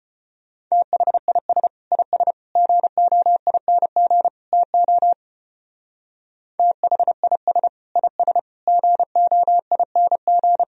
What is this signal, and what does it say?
Morse code, heard as a keyed tone, about 32 wpm, 700 Hz. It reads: THIS IS GOING TO THIS IS GOING